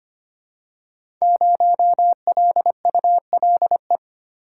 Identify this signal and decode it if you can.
Morse code, heard as a keyed tone, about 25 wpm, 700 Hz. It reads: 0LULE
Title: Code Groups - 5 Letter:Numbers - 25wpm